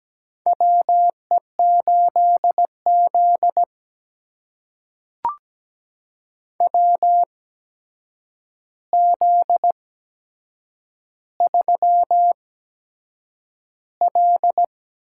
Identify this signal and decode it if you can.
Morse code, heard as a keyed tone, about 17 wpm, spaced as for 7 wpm, 700 Hz. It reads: WE8Z WZ3L
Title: USA Call Signs - 4 Characters - 8x Character Spacing - 17wpm